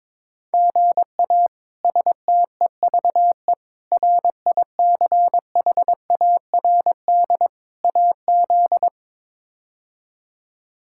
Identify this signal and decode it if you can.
Morse code, heard as a keyed tone, about 22 wpm, 700 Hz. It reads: GA STEVE RICHARD AZ
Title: K1USN SST QSO Practice - 22wpm